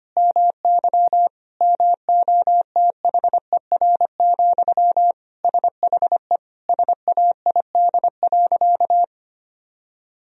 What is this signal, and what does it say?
Morse code, heard as a keyed tone, about 25 wpm, 700 Hz. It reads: MY MOTHER, SHE SAID.
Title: Sentences from Top 1000 Words - Review Part 1 - 25wpm